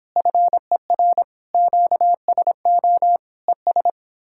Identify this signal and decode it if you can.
Morse code, heard as a keyed tone, about 29 wpm, 700 Hz. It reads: FER QSO ES